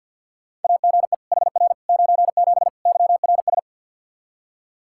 Morse code, heard as a keyed tone, about 48 wpm, 700 Hz. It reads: AGE HR 97 YRS
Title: Practice QSO - Exchange 3 - More Info - 50wpm